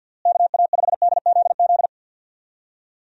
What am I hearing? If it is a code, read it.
Morse code, heard as a keyed tone, about 52 wpm, 700 Hz. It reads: KA5DZZ
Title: USA Call Signs - 6 Characters - Repeated 2x - 50wpm